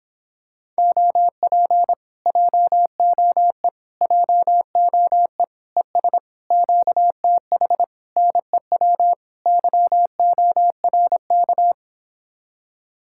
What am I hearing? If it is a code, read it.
Morse code, heard as a keyed tone, about 26 wpm, 700 Hz. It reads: OP JOE JOE ES QTH NEW YORK